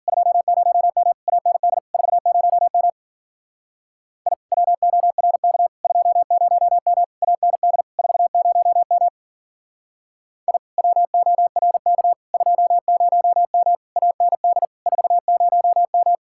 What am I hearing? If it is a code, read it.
Morse code, heard as a keyed tone, about 49 wpm, 700 Hz. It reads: 20M AND 40M I WORK 20M AND 40M I WORK 20M AND 40M
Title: Speed-Racing Common QSO Phrases - 40wpm